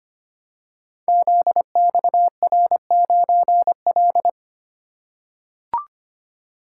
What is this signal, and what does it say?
Morse code, heard as a keyed tone, about 25 wpm, 700 Hz. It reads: ZXR9L
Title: Code Groups - 5 Letter:Numbers - 25wpm